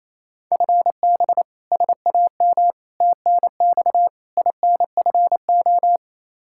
Morse code, heard as a keyed tone, about 28 wpm, 700 Hz. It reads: FB SAM TNX INFO
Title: Practice QSO - Exchange 3 - More Info - 28wpm